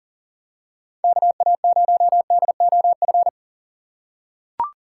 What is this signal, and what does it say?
Morse code, heard as a keyed tone, about 38 wpm, 700 Hz. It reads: KA0DOF
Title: USA Call Signs - 3x Character Spacing - 40wpm